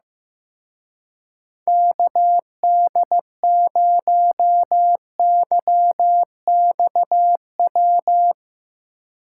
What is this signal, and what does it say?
Morse code, heard as a keyed tone, about 15 wpm, 700 Hz. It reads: KD0YXW
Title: Speed Racing - USA 2x3 Call Signs 15wpm